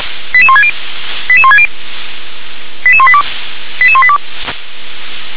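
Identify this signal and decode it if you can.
Five-tone selective call (ZVEI), heard as quick runs of five tones, tones of about 70 ms. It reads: 88160 88160 88182 88182
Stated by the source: ZVEI five-tone call received on an SDR.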